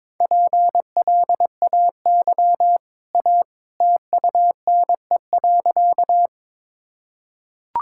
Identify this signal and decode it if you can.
Morse code, heard as a keyed tone, about 22 wpm, 700 Hz. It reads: PLAY A TUNE.
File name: practice / 22wpm / Easy Sentences with Hints - 22wpm